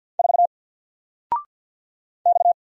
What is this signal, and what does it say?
Morse code, heard as a keyed tone, about 50 wpm, 700 Hz. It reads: 4 X
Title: Single Letter-Number 50wpm